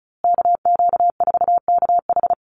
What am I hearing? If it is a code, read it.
Morse code, heard as a keyed tone, about 34 wpm, 700 Hz. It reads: KQ4KH
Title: USA Call Signs Repeated 2x 35wpm